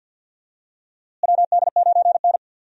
Morse code, heard as a keyed tone, about 51 wpm, 700 Hz. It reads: WD9N
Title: USA Call Signs - 4 Characters - Repeated 3x - 50wpm